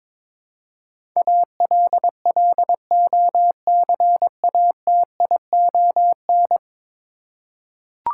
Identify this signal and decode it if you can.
Morse code, heard as a keyed tone, about 22 wpm, 700 Hz. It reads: ALLOCATION
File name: practice / 22wpm / 10 Letter Words - First 3 Letters Given - 22wpm